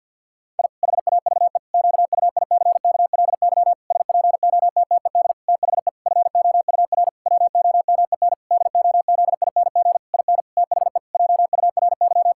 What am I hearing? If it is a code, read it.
Morse code, heard as a keyed tone, about 50 wpm, 700 Hz. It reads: I HAVE QUICKLY SPOTTED THE FOUR WOMEN DOZING IN THE JURY
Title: Pangram Challenge v2 - 50wpm